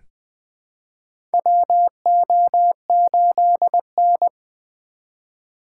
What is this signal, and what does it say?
Morse code, heard as a keyed tone, about 20 wpm, 700 Hz. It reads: WO8N
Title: USA Call Signs - 4 Characters - 2x Character Spacing - 20wpm